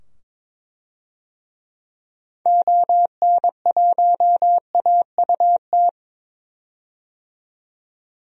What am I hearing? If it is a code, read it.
Morse code, heard as a keyed tone, about 22 wpm, 700 Hz. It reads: ON1AUT
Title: DXCC-100 Call Signs - 6 Characters - 22wpm